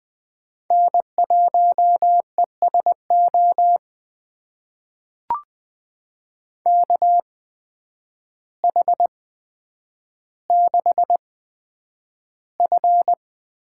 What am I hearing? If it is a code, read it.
Morse code, heard as a keyed tone, about 20 wpm, spaced as for 9 wpm, 700 Hz. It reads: N1ESO KH6F
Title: USA Call Signs - 5 Characters - 8x Character Spacing - 20wpm